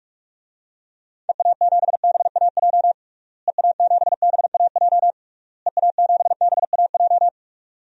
Morse code, heard as a keyed tone, about 46 wpm, 700 Hz. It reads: EA7BAJ EA7BAJ EA7BAJ
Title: DXCC-100 Call Signs - 6 Characters - Repeated 3x - 45wpm